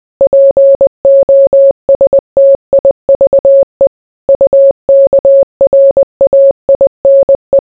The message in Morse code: POSTIVE UKRAINE

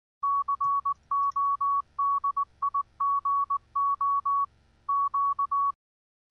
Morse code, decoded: CODIGO Q

